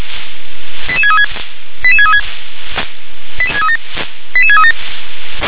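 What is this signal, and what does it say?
Five-tone selective call (ZVEI), heard as quick runs of five tones, tones of about 70 ms. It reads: 88636 88636 88637
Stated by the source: ZVEI five-tone call received on SDR.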